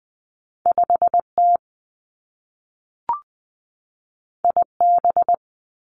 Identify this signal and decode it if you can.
Morse code, heard as a keyed tone, about 20 wpm, 700 Hz. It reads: 5 T I B
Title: Two Challenging Letter-Number Combinations 20wpm